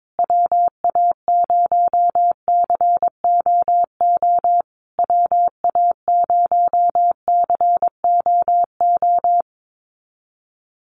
Morse code, heard as a keyed tone, about 22 wpm, 700 Hz. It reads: WA0COO WA0COO
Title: USA Call Signs Repeated 3x 22wpm